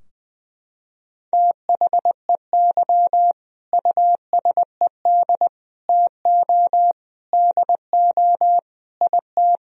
Morse code, heard as a keyed tone, about 20 wpm, 700 Hz. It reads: THEY USED TO DO IT
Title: Sentences from Top 1000 Words - Review Part 4 - 20wpm